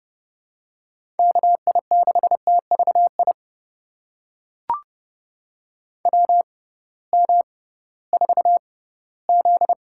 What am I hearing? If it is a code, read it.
Morse code, heard as a keyed tone, about 30 wpm, spaced as for 18 wpm, 700 Hz. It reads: KI6TVI WM4Z
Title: USA Call Signs - 6x Character Spacing - 30wpm